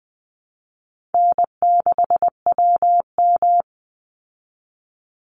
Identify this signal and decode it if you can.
Morse code, heard as a keyed tone, about 19 wpm, 700 Hz. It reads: N6WM
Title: Easy Call Signs Repeated 3x - 20wpm